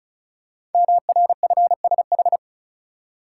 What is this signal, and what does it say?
Morse code, heard as a keyed tone, about 36 wpm, 700 Hz. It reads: MRFSH
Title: Code Groups - 5 Letters - 35wpm